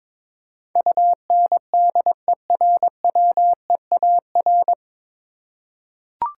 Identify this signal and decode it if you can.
Morse code, heard as a keyed tone, about 22 wpm, 700 Hz. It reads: UNDERWEAR
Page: Morse Code Ninja practice 22wpm